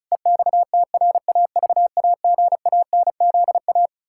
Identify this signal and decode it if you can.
Morse code, heard as a keyed tone, about 34 wpm, 700 Hz. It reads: EXTRAVAGANZA
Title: Pangram Challenge v2 - 35wpm